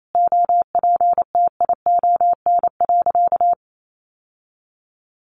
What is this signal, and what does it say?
Morse code, heard as a keyed tone, about 28 wpm, 700 Hz. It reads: OPTION.
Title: Sentences from Top 1300 Words 28wpm